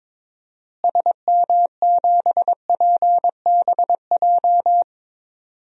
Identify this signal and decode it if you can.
Morse code, heard as a keyed tone, about 21 wpm, 700 Hz. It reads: SM7PBJ
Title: DXCC-100 Call Signs - 6 Characters - Repeated 2x - 22wpm